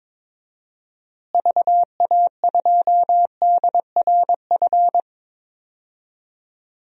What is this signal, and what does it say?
Morse code, heard as a keyed tone, about 21 wpm, 700 Hz. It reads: VA2DRF
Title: DXCC-100 Call Signs - 6 Characters - Repeated 3x - 22wpm